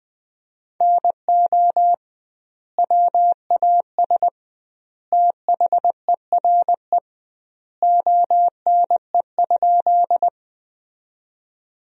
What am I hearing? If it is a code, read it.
Morse code, heard as a keyed tone, about 20 wpm, 700 Hz. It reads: NO WAS THERE ONE?
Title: 2x Word-Spacing - Sentences from Top 100 Words - 20wpm